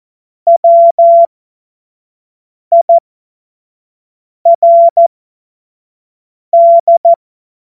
Morse code, heard as a keyed tone, about 14 wpm, spaced as for 5 wpm, 680 Hz. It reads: WIRD
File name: Morsekurs / Witze / Witz 07.November 2023 25 F70